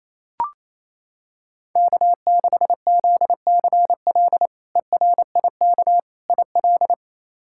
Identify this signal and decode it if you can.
Morse code, heard as a keyed tone, about 28 wpm, 700 Hz. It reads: K6ZCL ERIK IL
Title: K1USN SST QSO Practice - 28wpm